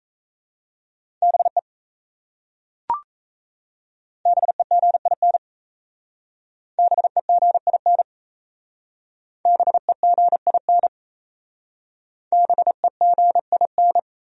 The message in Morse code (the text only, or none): BE BEGIN BEGIN BEGIN BEGIN